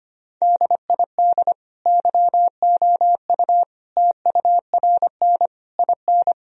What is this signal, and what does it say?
Morse code, heard as a keyed tone, about 25 wpm, 700 Hz. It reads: DID YOU TURN IN
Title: Sentences from Top 1000 Words - Review Part 2 - 25wpm